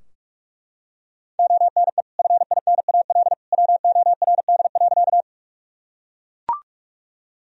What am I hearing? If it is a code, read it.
Morse code, heard as a keyed tone, about 42 wpm, 700 Hz. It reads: ONE FINAL WORD.
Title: Sentences from Top 1000 Words - Review Part 2 - 45wpm